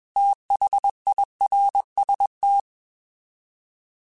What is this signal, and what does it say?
Morse code, heard as a keyed tone, about 21 wpm, 800 Hz. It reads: THIRST
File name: TH / TH mp3s / Fast TH @20